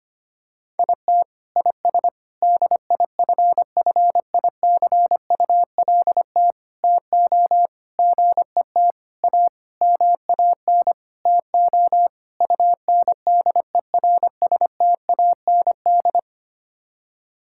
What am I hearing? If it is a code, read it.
Morse code, heard as a keyed tone, about 25 wpm, 700 Hz. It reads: IT IS DIFFICULT TO GET A MAN TO UNDERSTAND
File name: Sets of 9 Words - 25wpm